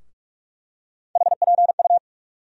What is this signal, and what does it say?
Morse code, heard as a keyed tone, about 45 wpm, 700 Hz. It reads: HPU
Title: USA Call Sign Postfixes - 3 Letters 45wpm